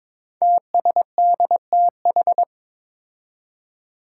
Morse code, heard as a keyed tone, about 22 wpm, 700 Hz. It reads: TSDTH
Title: Five Challenging Letter-Number Combinations 22wpm